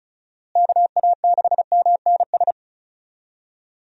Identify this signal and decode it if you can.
Morse code, heard as a keyed tone, about 36 wpm, 700 Hz. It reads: KA6MNS
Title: USA Call Signs - 6 Characters - Repeated 3x - 35wpm